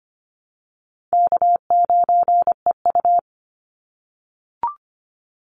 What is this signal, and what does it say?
Morse code, heard as a keyed tone, about 25 wpm, 700 Hz. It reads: K9EU
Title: Call Signs - 25wpm